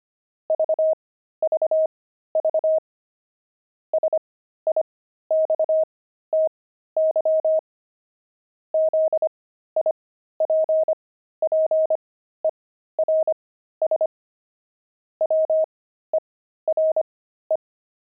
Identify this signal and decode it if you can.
Morse code, heard as a keyed tone, about 25 wpm, spaced as for 13 wpm, 630 Hz. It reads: VVV SIXTY ZIPPERS WERE
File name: pangram-fs-13-cs-25-sixty